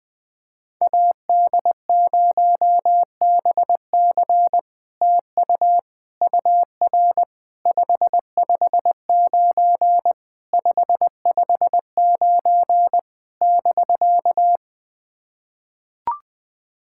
Morse code, heard as a keyed tone, about 20 wpm, 700 Hz. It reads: AD0BC TU UR 559 559 <BK>
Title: POTA QSO Practice - 20wpm